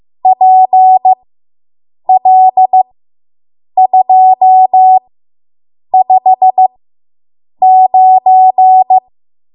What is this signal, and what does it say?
Morse code, heard as a keyed tone, about 15 wpm, spaced as for 7 wpm, 750 Hz. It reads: PL259